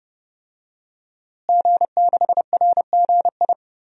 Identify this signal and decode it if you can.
Morse code, heard as a keyed tone, about 30 wpm, 700 Hz. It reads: G6RGI
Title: DXCC-100 Call Signs - 5 Characters - Repeated 2x - 30wpm